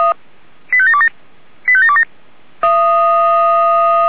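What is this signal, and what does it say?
Five-tone selective call (ZVEI), heard as quick runs of five tones, tones of about 70 ms. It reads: 85727 85727
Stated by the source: ZVEI five-tone call 85726 received via SDR.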